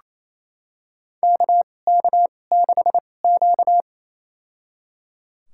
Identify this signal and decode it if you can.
Morse code, heard as a keyed tone, about 28 wpm, spaced as for 20 wpm, 700 Hz. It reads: KK6Q
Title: USA Call Signs - 2x Character Spacing - 28wpm